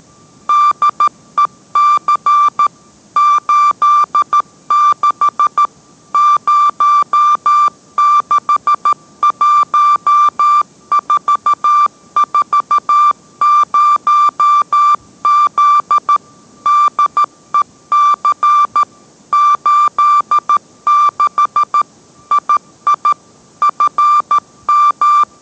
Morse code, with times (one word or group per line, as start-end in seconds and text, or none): 0.5-2.7 DEC
3.2-5.7 86
6.1-16.2 061440Z
16.7-18.8 DEC
19.3-21.8 86
22.3-23.1 II
23.6-25.2 FM